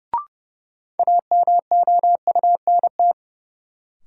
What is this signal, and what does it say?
Morse code, heard as a keyed tone, about 31 wpm, 700 Hz. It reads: AMOUNT
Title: Sets of 3 Words - New Word Drill - 30wpm